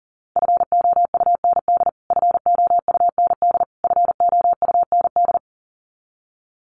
Morse code, heard as a keyed tone, about 40 wpm, 700 Hz. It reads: FOUND FOUND FOUND
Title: Top 500 Words Repeated 40wpm